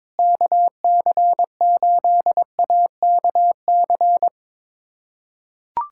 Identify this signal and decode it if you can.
Morse code, heard as a keyed tone, about 22 wpm, 700 Hz. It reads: KC8AKC
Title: USA Callsign DE Callsign - 6 Characters - 22wpm